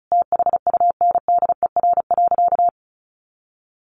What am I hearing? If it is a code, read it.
Morse code, heard as a keyed tone, about 34 wpm, 700 Hz. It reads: THUNDER.
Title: A Sound Of Thunder 35wpm